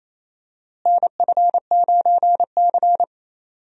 Morse code, heard as a keyed tone, about 28 wpm, 700 Hz. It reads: NF9C
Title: USA Call Signs - 4 Characters - Repeated 3x - 28wpm